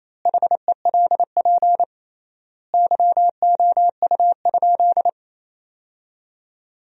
Morse code, heard as a keyed tone, about 28 wpm, 700 Hz. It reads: HELP YOU?
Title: 3x Word-Spacing - Sentences from Top 200 Words - 28wpm